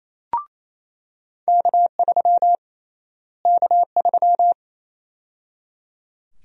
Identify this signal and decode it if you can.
Morse code, heard as a keyed tone, about 28 wpm, 700 Hz. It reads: K3 K3
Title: USA Call Sign Prefixes - Repeated 2x - 28wpm